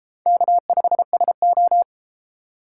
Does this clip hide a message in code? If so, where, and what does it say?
Yes: Morse code from 0.3 to 1.8 s, heard as a keyed tone, about 30 wpm, 700 Hz. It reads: K5SO